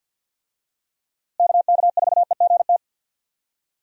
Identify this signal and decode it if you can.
Morse code, heard as a keyed tone, about 53 wpm, 700 Hz. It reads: KK4EGT
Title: USA Call Signs - 6 Characters - Repeated 3x - 50wpm